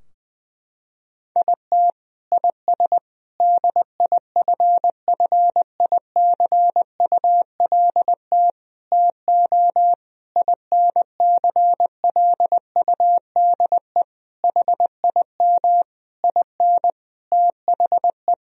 Morse code, heard as a keyed tone, about 21 wpm, 700 Hz. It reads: IT IS DIFFICULT TO INCLUDE HIM IN THE
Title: Sentences from Top 1000 Words - Review Part 3 - 20wpm